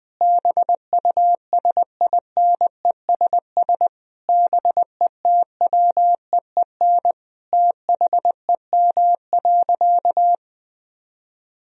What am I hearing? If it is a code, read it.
Morse code, heard as a keyed tone, about 20 wpm, 700 Hz. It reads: BUSINESS BETWEEN THEM.
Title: Sentences from Top 1000 Words - Review Part 1 - 20wpm